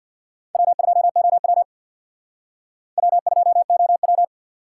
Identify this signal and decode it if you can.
Morse code, heard as a keyed tone, about 60 wpm, 700 Hz. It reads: W2OW W2OW
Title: Speed Racing - USA 1x2 Call Signs 50wpm